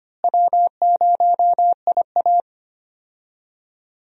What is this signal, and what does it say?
Morse code, heard as a keyed tone, about 25 wpm, 700 Hz. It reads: W0IA
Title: USA Call Signs - 4 Characters - Repeated 3x - 25wpm